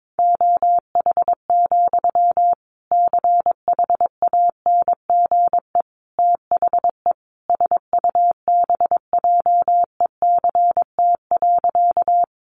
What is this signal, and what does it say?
Morse code, heard as a keyed tone, about 22 wpm, 700 Hz. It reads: OH, CHANGE THE SUBJECT.
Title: Sentences from Top 800 Words 22wpm